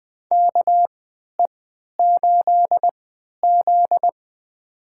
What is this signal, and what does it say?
Morse code, heard as a keyed tone, about 20 wpm, spaced as for 11 wpm, 700 Hz. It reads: KE8Z
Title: USA Call Signs - 3x Character Spacing - 20wpm